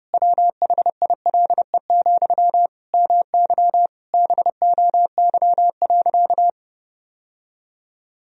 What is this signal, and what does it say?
Morse code, heard as a keyed tone, about 32 wpm, 700 Hz. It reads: WHILE, MY BOY.
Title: Sentences from Top 1000 Words - Review Part 1 - 30wpm